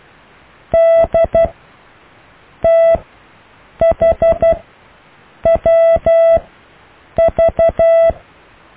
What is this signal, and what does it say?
Morse code, heard as a keyed tone, about 12 wpm, spaced as for 7 wpm, 660 Hz. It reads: DTHWV
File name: cw / EKTPV+BFHNW